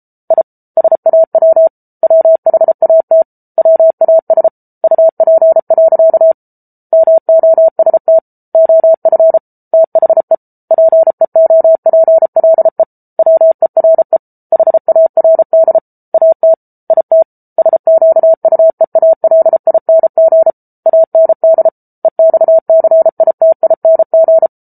Morse code, heard as a keyed tone, about 33 wpm, 650 Hz. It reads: I SAW WHAT WAS UP. MOST OF THE PEOPLE WERE HARD AT IT SQUEALING AND EXCITING